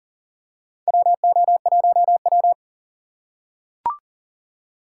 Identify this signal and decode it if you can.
Morse code, heard as a keyed tone, about 40 wpm, 700 Hz. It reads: WO1W
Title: USA Call Signs - 4 Characters - 2x Character Spacing - 40wpm